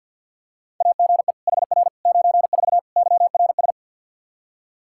Morse code, heard as a keyed tone, about 53 wpm, 700 Hz. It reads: AGE HR 94 YRS
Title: Practice QSO - Exchange 3 - More Info - 50wpm